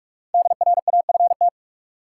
Morse code, heard as a keyed tone, about 44 wpm, 700 Hz. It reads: DRAFT
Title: Five Letter Words - 45wpm